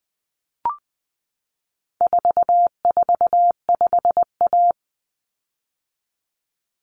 Morse code, heard as a keyed tone, about 20 wpm, 700 Hz. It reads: 445A
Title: Common Cut Numbers - 4-Digit 20wpm